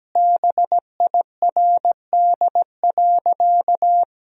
Morse code, heard as a keyed tone, about 17 wpm, 700 Hz. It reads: BIRD.
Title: Sentences from Top 1000 Words - Review Part 2 - 17wpm